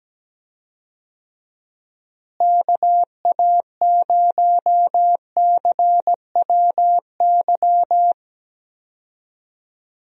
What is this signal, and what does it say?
Morse code, heard as a keyed tone, about 17 wpm, 700 Hz. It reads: KA0CWY
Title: USA Call Signs - 6 Characters - Repeated 3x - 17wpm